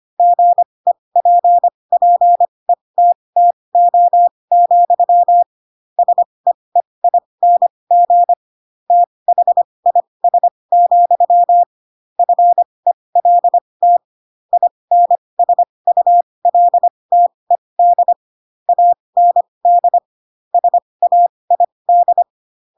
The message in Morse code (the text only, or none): GEPPETTO, SEEING THIS, FELT INSULTED AND SAID